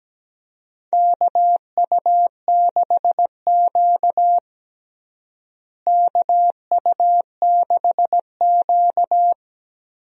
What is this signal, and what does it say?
Morse code, heard as a keyed tone, about 17 wpm, 700 Hz. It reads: KU6Q KU6Q
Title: USA Call Signs - 4 Characters - Repeated 3x - 17wpm